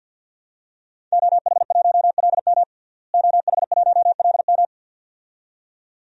Morse code, heard as a keyed tone, about 52 wpm, 700 Hz. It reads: OH1LM OH1LM
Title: DXCC-100 Call Signs - 5 Characters - Repeated 2x - 50wpm